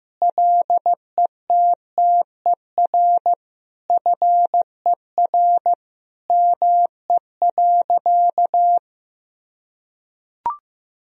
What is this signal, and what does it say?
Morse code, heard as a keyed tone, about 16 wpm, 700 Hz. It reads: LETTER FER ME.
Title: Sentences from Top 300 Words and Abbreviations 15wpm